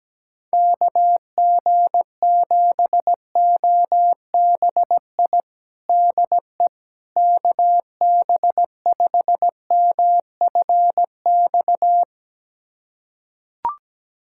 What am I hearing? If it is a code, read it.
Morse code, heard as a keyed tone, about 17 wpm, 700 Hz. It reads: KG7OBI DE KB5MFX
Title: USA Callsign DE Callsign - 6 Characters - 17wpm